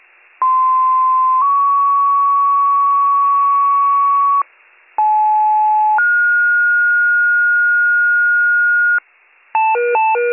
Two-tone page, heard as a one-second tone, then a three-second tone, about 1040 then 1120 Hz, about 860 then 1430 Hz. It is heard from 0.4 to 9.0 s.